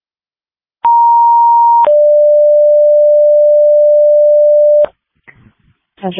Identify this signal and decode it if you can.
Two-tone page, heard as a one-second tone, then a three-second tone, about 940 then 590 Hz, with speech. It is heard from 0.8 to 4.9 s.